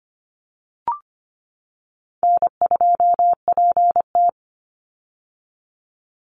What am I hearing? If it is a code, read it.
Morse code, heard as a keyed tone, about 25 wpm, 700 Hz. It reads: N2PT
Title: USA Call Signs 25wpm